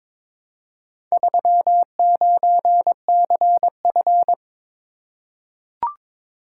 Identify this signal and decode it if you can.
Morse code, heard as a keyed tone, about 22 wpm, 700 Hz. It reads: 39CF
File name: practice / 22wpm / Four Letter-Number Combinations - 22wpm